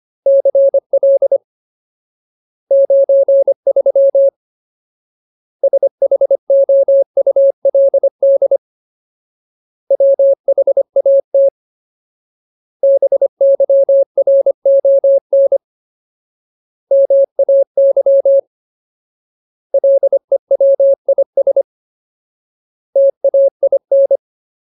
Morse code, heard as a keyed tone, about 25 wpm, 550 Hz. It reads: CL 93 SHOULD WHAT BYRON MAY LEWIS TAIN
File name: Ultimate Word List / 25WPM Ultimate